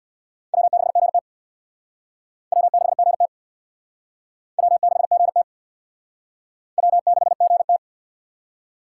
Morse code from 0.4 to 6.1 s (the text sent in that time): W6GT W6GT W6GT